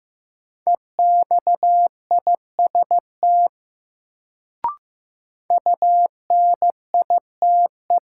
Morse code, heard as a keyed tone, about 15 wpm, 700 Hz. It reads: EXIST UNITE
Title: Sentences from Top 1000 Words - Review Part 5 - New Word Drill - 15wpm